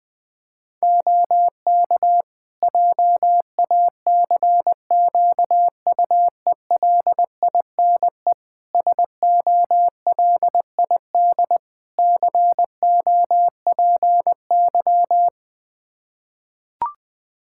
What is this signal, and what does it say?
Morse code, heard as a keyed tone, about 20 wpm, 700 Hz. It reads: OK JACQUELINE SOLID COPY
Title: K1USN SST QSO Practice 20wpm